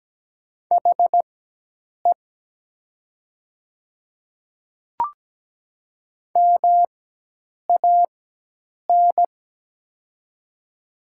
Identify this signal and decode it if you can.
Morse code, heard as a keyed tone, about 17 wpm, spaced as for 8 wpm, 700 Hz. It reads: HE MAN
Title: 4x Character Spacing - Top 200 Words - 17wpm